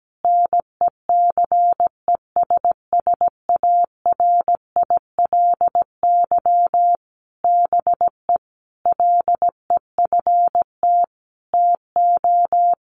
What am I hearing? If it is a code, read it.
Morse code, heard as a keyed tone, about 17 wpm, 700 Hz. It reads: NECESSARILY BE LEFT TO
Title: Sentences from Top 1400 Words 17wpm